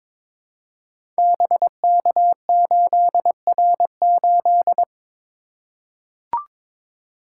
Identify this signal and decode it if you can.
Morse code, heard as a keyed tone, about 22 wpm, 700 Hz. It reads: BK8R8
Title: Code Groups - 5 Letter:Numbers - 22wpm